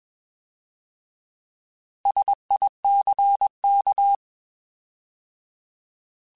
Morse code, heard as a keyed tone, about 21 wpm, 800 Hz. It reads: SICK
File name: Alphabetical / Alphabetical mp3s / Ss @20